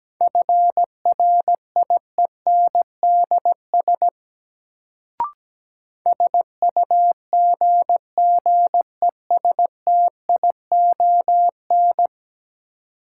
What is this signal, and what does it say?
Morse code, heard as a keyed tone, about 17 wpm, 700 Hz. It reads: FRIENDS SUGGESTION
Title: Sentences from Top 1600 Words - New Word Drill Alt - 17wpm